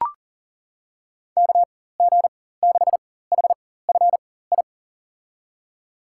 Morse code, heard as a keyed tone, about 40 wpm, spaced as for 19 wpm, 700 Hz. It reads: KG6HFI
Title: USA Call Signs - 4x Character Spacing - 40wpm